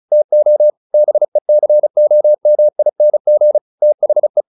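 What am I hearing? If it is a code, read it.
Morse code, heard as a keyed tone, about 34 wpm, 600 Hz. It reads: TO BECOMING THE
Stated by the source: Morse practice at 35 wpm from CW med Gnister, sending Ken Follett's novel Never in English.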